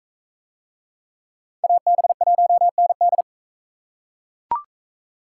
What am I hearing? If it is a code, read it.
Morse code, heard as a keyed tone, about 39 wpm, 700 Hz. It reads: AB1ND